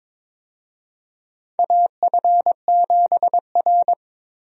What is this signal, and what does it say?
Morse code, heard as a keyed tone, about 21 wpm, 700 Hz. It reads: AF7R
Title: USA Call Signs - 4 Characters - Repeated 2x - 22wpm